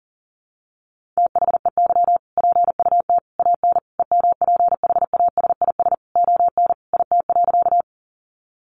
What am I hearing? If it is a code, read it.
Morse code, heard as a keyed tone, about 40 wpm, 700 Hz. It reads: THEY PUT AN EMPHASIS ON IT.
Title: Sentences from Top 1400 Words 40wpm